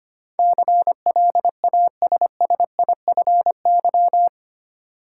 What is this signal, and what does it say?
Morse code, heard as a keyed tone, about 25 wpm, 700 Hz. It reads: CLASSIFY
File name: Eight Letter Words - 25wpm